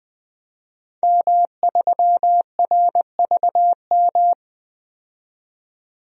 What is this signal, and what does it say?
Morse code, heard as a keyed tone, about 21 wpm, 700 Hz. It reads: M3RVM
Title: DXCC-100 Call Signs - 5 Characters - 20wpm